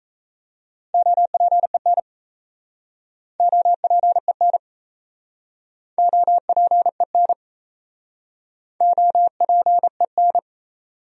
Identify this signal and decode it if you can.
Morse code, heard as a keyed tone, about 37 wpm, 700 Hz. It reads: OPEN OPEN OPEN OPEN